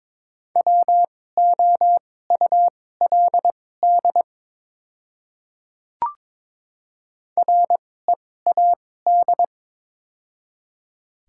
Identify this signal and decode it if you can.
Morse code, heard as a keyed tone, about 22 wpm, spaced as for 16 wpm, 700 Hz. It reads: WOULD READ
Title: 2x Character Spacing - Top 300 Words - 22wpm